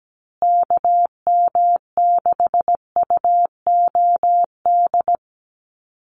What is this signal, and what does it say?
Morse code, heard as a keyed tone, about 17 wpm, 700 Hz. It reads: KM6UOD
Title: USA Call Signs Repeated 2x 17wpm